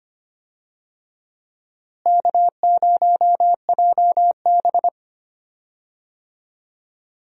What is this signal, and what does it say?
Morse code, heard as a keyed tone, about 25 wpm, 700 Hz. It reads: K0JB
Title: USA Call Signs - 4 Characters - Repeated 3x - 25wpm